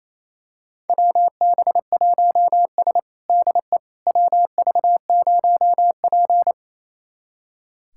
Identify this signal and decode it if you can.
Morse code, heard as a keyed tone, about 28 wpm, 700 Hz. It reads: WB1S DE WV0P
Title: USA Callsign DE Callsign - 4 Characters - 28wpm